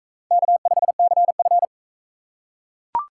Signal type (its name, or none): Morse code